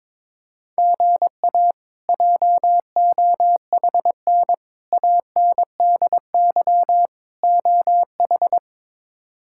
Morse code, heard as a keyed tone, about 22 wpm, 700 Hz. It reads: GA JOHN ANDY OH
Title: K1USN SST QSO Practice - 22wpm